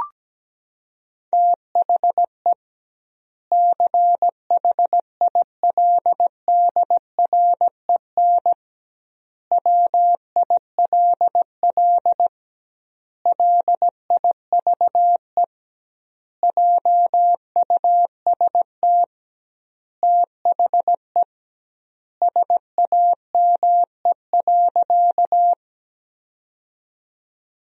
Morse code, heard as a keyed tone, about 17 wpm, 700 Hz. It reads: THE CHILDREN WILL LIVE JUST THE SAME.
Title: 2x Word-Spacing - Sentences from Top 300 Words - 17wpm